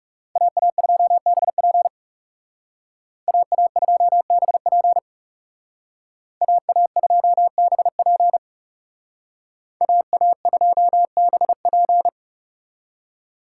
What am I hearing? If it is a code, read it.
Morse code, heard as a keyed tone, about 38 wpm, 700 Hz. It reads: AA2BP AA2BP AA2BP AA2BP